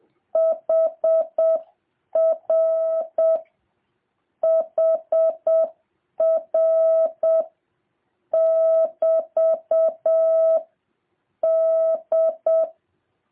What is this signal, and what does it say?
Morse code, heard as a keyed tone, about 7 wpm, 650 Hz. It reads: HR HR <BT> D